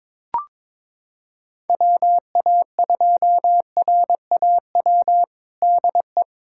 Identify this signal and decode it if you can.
Morse code, heard as a keyed tone, about 22 wpm, 700 Hz. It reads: WA2RAW DE
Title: USA Callsign DE Callsign - 6 Characters - 22wpm